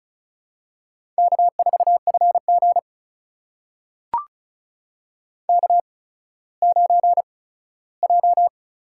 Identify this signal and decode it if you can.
Morse code, heard as a keyed tone, about 36 wpm, spaced as for 17 wpm, 700 Hz. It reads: K4FG K9J
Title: USA Call Signs - 4 Characters - 8x Character Spacing - 35wpm